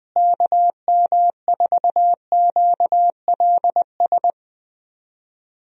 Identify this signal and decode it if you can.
Morse code, heard as a keyed tone, about 20 wpm, 700 Hz. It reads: KM4QLS
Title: USA Call Signs - 6 Characters - Repeated 2x - 20wpm